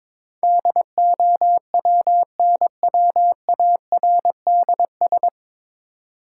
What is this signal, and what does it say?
Morse code, heard as a keyed tone, about 22 wpm, 700 Hz. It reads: DOWNWARDS
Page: Morse Code Ninja practice 22wpm